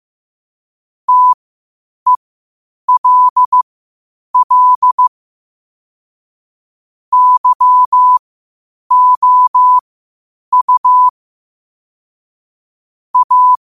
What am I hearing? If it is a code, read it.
Morse code, heard as a keyed tone, about 15 wpm, spaced as for 8 wpm, 1000 Hz. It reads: TELL YOU A
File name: qotdfiles / 8wpm